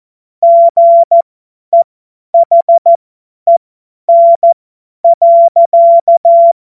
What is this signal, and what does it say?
Morse code, heard as a keyed tone, about 14 wpm, spaced as for 10 wpm, 680 Hz. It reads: GEHEN.
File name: Morsekurs / Witze / Witz 22. Februar 2022 50 F70